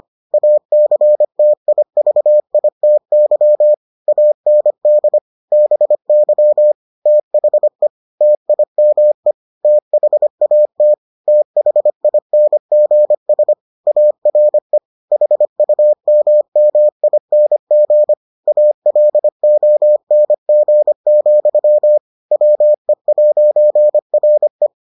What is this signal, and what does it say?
Morse code, heard as a keyed tone, about 25 wpm, 600 Hz. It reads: ACTIVITY AND BY THE TIME THAT THINGS ARE HUMMING ALONG, WE'RE